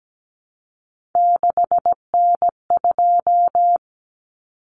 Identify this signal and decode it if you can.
Morse code, heard as a keyed tone, about 17 wpm, 700 Hz. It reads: 6N2
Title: Common Cut Numbers - 3-Digit 17wpm